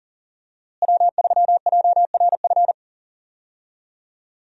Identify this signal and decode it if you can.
Morse code, heard as a keyed tone, about 40 wpm, 700 Hz. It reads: W3JRF
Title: USA Call Signs - 5 Characters - Repeated 2x - 40wpm